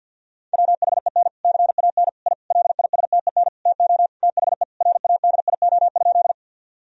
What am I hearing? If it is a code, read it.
Morse code, heard as a keyed tone, about 46 wpm, 700 Hz. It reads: WHEN CAN I LISTEN TO THE RADIO?
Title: Sentences from Top 1000 Words - Review Part 3 - 50wpm